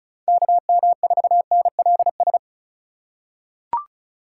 Morse code, heard as a keyed tone, about 36 wpm, 700 Hz. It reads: KM4NLS
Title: USA Call Signs - 6 Characters - 2x Character Spacing - 35wpm